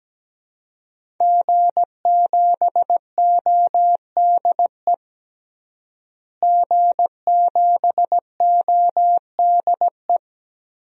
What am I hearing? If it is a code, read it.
Morse code, heard as a keyed tone, about 17 wpm, 700 Hz. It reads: G7ODE G7ODE
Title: DXCC-100 Call Signs - 5 Characters - Repeated 2x - 17wpm